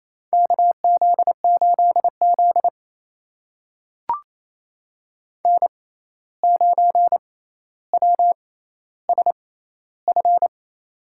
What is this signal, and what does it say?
Morse code, heard as a keyed tone, about 28 wpm, spaced as for 13 wpm, 700 Hz. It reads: KZ8Z N9WSF
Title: USA Call Signs - 6x Character Spacing - 28wpm